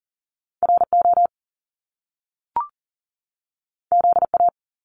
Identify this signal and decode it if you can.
Morse code, heard as a keyed tone, about 40 wpm, 700 Hz. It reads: RO ZA
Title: Top Two Letter Combinations in Words 40wpm